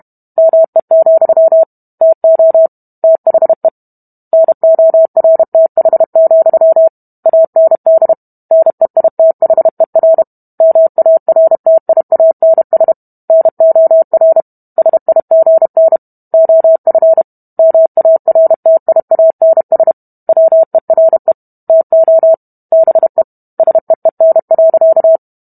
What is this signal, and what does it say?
Morse code, heard as a keyed tone, about 32 wpm, 650 Hz. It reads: ME, TO THE NORTH, AND NEITHER MARTIANS NOR SIGN OF MARTIANS WERE TO BE SEEN.